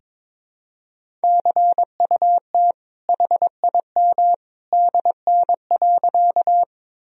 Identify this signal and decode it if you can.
Morse code, heard as a keyed tone, about 22 wpm, 700 Hz. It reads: CUT HIM DN.
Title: Sentences from Top 300 Words and Abbreviations 22wpm